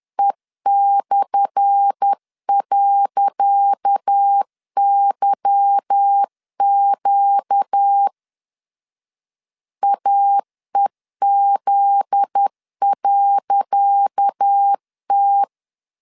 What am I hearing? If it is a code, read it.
Morse code, heard as a keyed tone, about 10 wpm, 800 Hz. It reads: E/.YQ AEZ.T
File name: LEZIONI / 10 SIGNS / 10C DOT 10wpm 2940